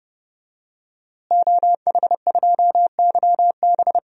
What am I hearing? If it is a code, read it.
Morse code, heard as a keyed tone, about 30 wpm, 700 Hz. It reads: OH2YB
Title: DXCC-100 Call Signs - 5 Characters - Repeated 2x - 30wpm